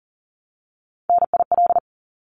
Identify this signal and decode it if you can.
Morse code, heard as a keyed tone, about 40 wpm, 700 Hz. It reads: NIL
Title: Abbreviations 40wpm